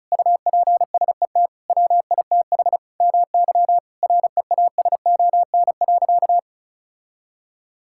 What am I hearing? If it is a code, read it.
Morse code, heard as a keyed tone, about 36 wpm, 700 Hz. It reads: UPSET WITH MY REASON.
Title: Sentences from Top 1000 Words - Review Part 5 - 35wpm